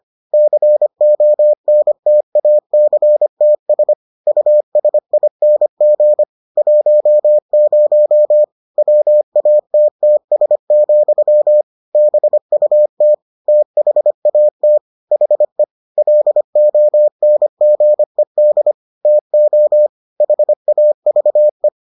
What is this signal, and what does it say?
Morse code, heard as a keyed tone, about 25 wpm, 600 Hz. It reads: CONTACTS USING 10 WATTS, BUT THAT HE LONGED TO HAVE